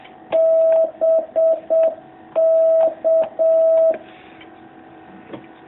Morse code, heard as a keyed tone, about 7 wpm, 660 Hz. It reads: BK
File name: jh8cbh / 2019cw